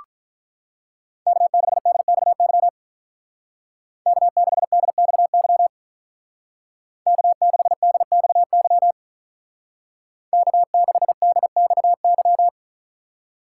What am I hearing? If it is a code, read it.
Morse code, heard as a keyed tone, about 43 wpm, 700 Hz. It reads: K6DXY K6DXY K6DXY K6DXY
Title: Speed Racing - USA 1x3 Call Signs 35wpm